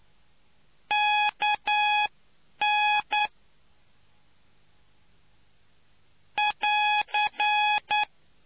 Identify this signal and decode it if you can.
Morse code, heard as a keyed tone, about 9 wpm, spaced as for 8 wpm, 840 Hz. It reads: KN <AR>